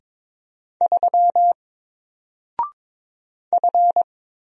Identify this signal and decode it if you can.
Morse code, heard as a keyed tone, about 22 wpm, 700 Hz. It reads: 3 F
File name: Single-Letter-Number-22wpm